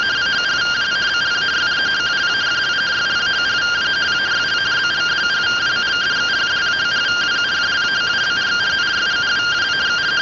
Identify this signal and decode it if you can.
Two-tone teleprinter signal, about 170 Hz shift.